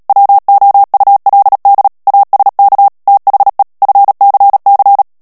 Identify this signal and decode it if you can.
Morse code, heard as a keyed tone, about 37 wpm, 800 Hz. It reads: WOULD ASK THE FCC
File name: morse mp3 / ANGLAIS texte clair 3 vitesse 35 mots minute